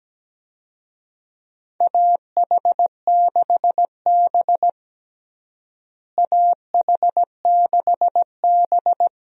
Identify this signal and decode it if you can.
Morse code, heard as a keyed tone, about 17 wpm, 700 Hz. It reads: AH6B AH6B
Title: USA Call Signs - 4 Characters - Repeated 2x - 17wpm